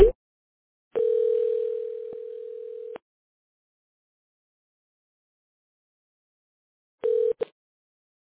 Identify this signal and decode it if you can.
Ringback tone (North America), 440 + 480 Hz.